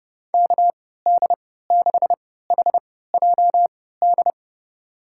Morse code, heard as a keyed tone, about 30 wpm, spaced as for 17 wpm, 700 Hz. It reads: KD6HJD
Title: USA Call Signs - 6 Characters - 3x Character Spacing - 30wpm